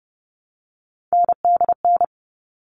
Morse code, heard as a keyed tone, about 30 wpm, 700 Hz. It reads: NDN
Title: All Cut Numbers - 3-Digit 30wpm